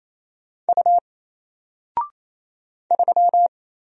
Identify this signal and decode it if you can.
Morse code, heard as a keyed tone, about 27 wpm, 700 Hz. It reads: U 3
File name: Single-Letter-Number-28wpm